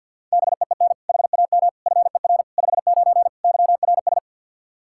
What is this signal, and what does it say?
Morse code, heard as a keyed tone, about 50 wpm, 700 Hz. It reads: BEEN HAM FER 59 YRS
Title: Practice QSO - Exchange 3 - More Info - 50wpm